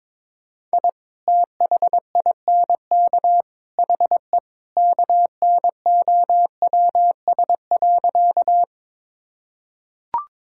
Morse code, heard as a keyed tone, about 22 wpm, 700 Hz. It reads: I THINK HE KNOWS.
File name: practice / 22wpm / Sentences with 3-Word Phrases - 22wpm